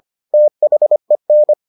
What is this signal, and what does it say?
Morse code, heard as a keyed tone, about 25 wpm, 600 Hz. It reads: THEN